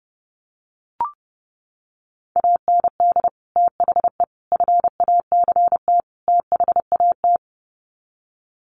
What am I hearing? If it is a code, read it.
Morse code, heard as a keyed tone, about 30 wpm, 700 Hz. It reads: AND THE FACT THAT
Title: Sets of 4 Words 30wpm